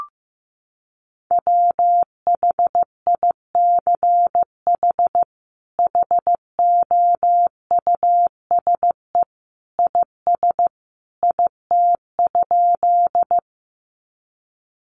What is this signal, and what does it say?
Morse code, heard as a keyed tone, about 15 wpm, 700 Hz. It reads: WHICH HOUSE IS IT?
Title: Sentences from Top 200 Words 15wpm